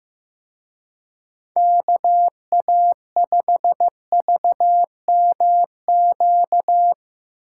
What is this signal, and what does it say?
Morse code, heard as a keyed tone, about 18 wpm, 700 Hz. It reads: KA5VMQ
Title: Speed Racing - USA 2x3 Call Signs 15wpm